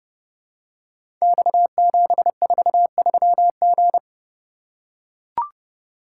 Morse code, heard as a keyed tone, about 30 wpm, 700 Hz. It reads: X743G
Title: Code Groups - 5 Letter:Numbers - 30wpm